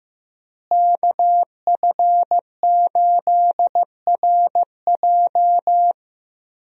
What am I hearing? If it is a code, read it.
Morse code, heard as a keyed tone, about 15 wpm, 700 Hz. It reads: KF8RJ
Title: USA Call Signs - 5 Characters - Repeated 3x - 15wpm